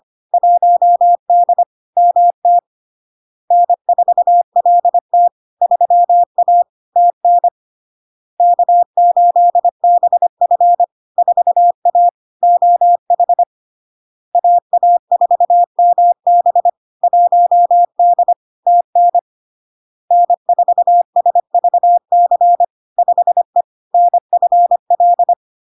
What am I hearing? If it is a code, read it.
Morse code, heard as a keyed tone, about 25 wpm, 700 Hz. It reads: 1D MT N4LT 3A TN K8BF 4A OH AA4MB 1D TN N4SVC 5E NFL
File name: Log-in-CW-25wpm-700-Hz